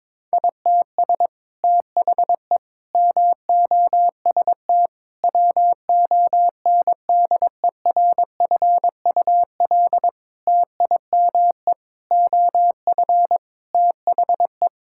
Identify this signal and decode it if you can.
Morse code, heard as a keyed tone, about 22 wpm, 700 Hz. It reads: ITS THE MOST WONDERFUL TIME OF THE
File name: practice / 22wpm / Sets of 8 Words - 22wpm